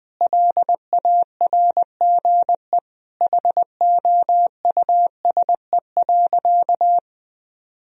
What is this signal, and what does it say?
Morse code, heard as a keyed tone, about 20 wpm, 700 Hz. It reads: LARGE HOUSE.
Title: Sentences from Top 1000 Words - Review Part 1 - 20wpm